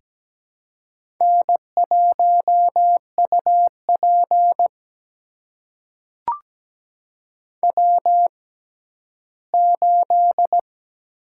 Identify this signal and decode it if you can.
Morse code, heard as a keyed tone, about 17 wpm, spaced as for 12 wpm, 700 Hz. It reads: N1UP W8
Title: USA Call Signs - 4 Characters - 6x Character Spacing - 17wpm